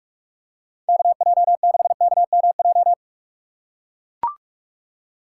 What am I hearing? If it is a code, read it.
Morse code, heard as a keyed tone, about 45 wpm, 700 Hz. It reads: KJ6KMJ